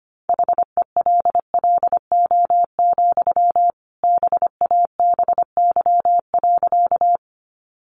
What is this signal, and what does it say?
Morse code, heard as a keyed tone, about 25 wpm, 700 Hz. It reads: HELLO, BABY.
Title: Sentences from Top 1300 Words 25wpm